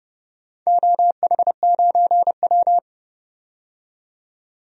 Morse code, heard as a keyed tone, about 31 wpm, 700 Hz. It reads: OH9W
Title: DXCC-100 Call Signs - 4 Characters - 30wpm